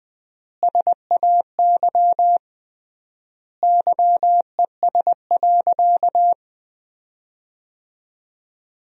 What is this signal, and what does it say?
Morse code, heard as a keyed tone, about 19 wpm, 700 Hz. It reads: SAY YES.
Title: 3x Word-Spacing - Sentences from Top 500 Words - 20wpm